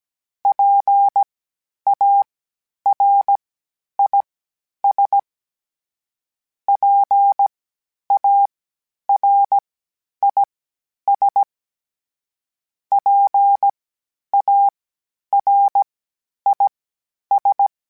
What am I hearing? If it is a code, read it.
Morse code, heard as a keyed tone, about 17 wpm, spaced as for 10 wpm, 800 Hz. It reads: PARIS PARIS PARIS
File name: PARIS Calibration / PARIS mp3s / PARIS@10-@16